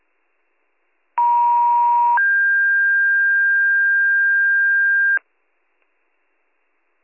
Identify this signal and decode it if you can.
Two-tone page, heard as a one-second tone, then a three-second tone, about 950 then 1600 Hz. It is heard from 1.2 to 5.2 s.